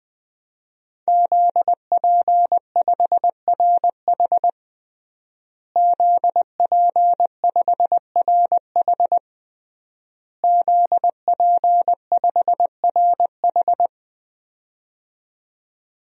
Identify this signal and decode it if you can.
Morse code, heard as a keyed tone, about 20 wpm, 700 Hz. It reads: ZP5RH ZP5RH ZP5RH
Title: DXCC-100 Call Signs - 5 Characters - Repeated 3x - 20wpm